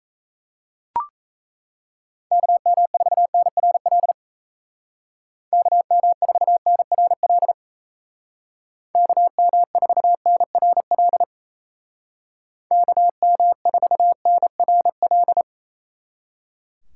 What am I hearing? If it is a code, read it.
Morse code, heard as a keyed tone, about 33 wpm, 700 Hz. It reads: KM4NRL KM4NRL KM4NRL KM4NRL
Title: Speed Racing - USA 2x3 Call Signs 28wpm